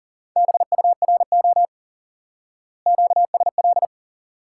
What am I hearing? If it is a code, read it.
Morse code, heard as a keyed tone, about 40 wpm, 700 Hz. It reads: BURO QSL
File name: IQE Course 3 - Lesson 89 - YAGI - Two Elements - 40wpm